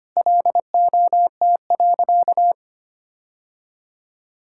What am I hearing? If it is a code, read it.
Morse code, heard as a keyed tone, about 25 wpm, 700 Hz. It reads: LOT.
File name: practice / 25wpm / Sentences with 3-Word Phrases - 25wpm